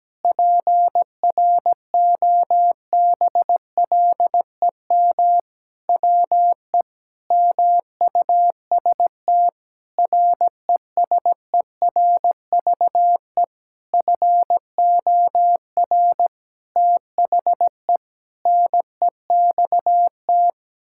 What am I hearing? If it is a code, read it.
Morse code, heard as a keyed tone, about 17 wpm, 700 Hz. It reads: PROBLEM WE MUST RESERVE FOR THE NEXT
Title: Sentences from Top 1000 Words - Review Part 6 - 17wpm